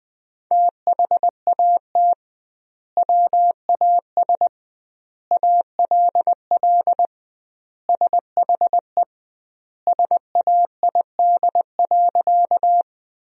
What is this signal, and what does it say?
Morse code, heard as a keyed tone, about 20 wpm, 700 Hz. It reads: THAT WAS ALL SHE SAID.
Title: 2x Word-Spacing - Sentences from Top 100 Words - 20wpm